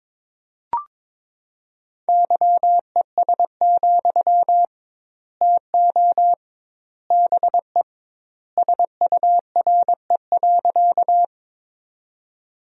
Morse code, heard as a keyed tone, about 21 wpm, 700 Hz. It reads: YES, TO BE SURE.
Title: 2x Word-Spacing - Sentences from Top 500 Words - 22wpm